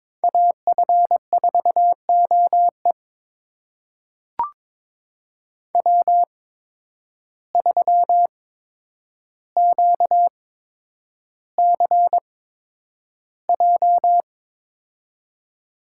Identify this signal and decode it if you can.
Morse code, heard as a keyed tone, about 22 wpm, spaced as for 9 wpm, 700 Hz. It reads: AF4OE W3QCJ
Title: USA Call Signs - 5 Characters - 8x Character Spacing - 22wpm